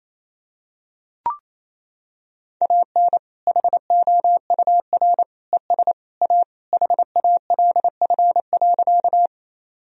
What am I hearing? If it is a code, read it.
Morse code, heard as a keyed tone, about 27 wpm, 700 Hz. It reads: AN HOUR ES A HALF.
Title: Sentences from Top 400 Words and Abbreviations 28wpm